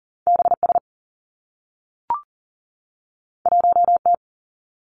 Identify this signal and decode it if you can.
Morse code, heard as a keyed tone, about 42 wpm, 700 Hz. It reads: BS 1T
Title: Two Challenging Letter-Number Combinations 40wpm